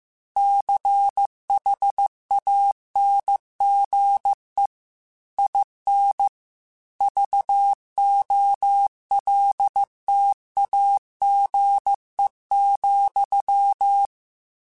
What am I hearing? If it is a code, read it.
Morse code, heard as a keyed tone, about 15 wpm, 800 Hz. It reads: CHANGE IN VOLTAGE,